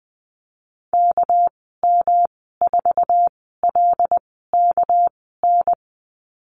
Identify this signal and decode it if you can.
Morse code, heard as a keyed tone, about 20 wpm, spaced as for 14 wpm, 700 Hz. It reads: KM4LKN
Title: 200 Lesson 32g - Letter k - Call Signs 20wpm